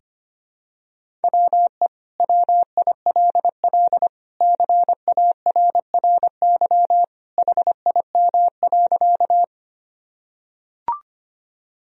Morse code, heard as a keyed tone, about 25 wpm, 700 Hz. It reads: WE WILL CARRY HIM.
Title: Sentences from Top 1000 Words - Review Part 1 - 25wpm